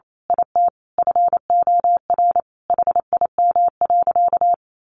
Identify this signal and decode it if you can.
Morse code, heard as a keyed tone, about 28 wpm, 700 Hz. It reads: IT FOR HIM.
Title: A Sound Of Thunder 28wpm